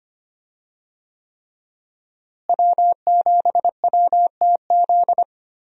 Morse code, heard as a keyed tone, about 25 wpm, 700 Hz. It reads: W7WTZ
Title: USA Call Signs - 5 Characters - Repeated 3x - 25wpm